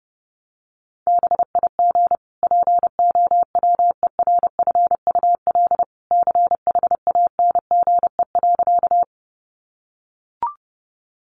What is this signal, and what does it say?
Morse code, heard as a keyed tone, about 30 wpm, 700 Hz. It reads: BIG POWERFUL CHANGE.
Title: Sentences from Top 1300 Words 30wpm